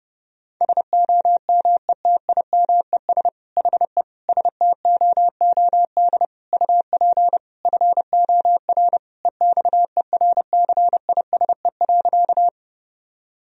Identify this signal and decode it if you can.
Morse code, heard as a keyed tone, about 30 wpm, 700 Hz. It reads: SOMETIMES HE STOOD UP FOR EXERCISE.
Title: Sentences from Top 1000 Words - Review Part 4 - 30wpm